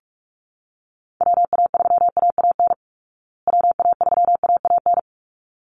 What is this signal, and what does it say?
Morse code, heard as a keyed tone, about 44 wpm, 700 Hz. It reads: WA3AAN WA3AAN
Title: Call Signs Repeated 3x - 45wpm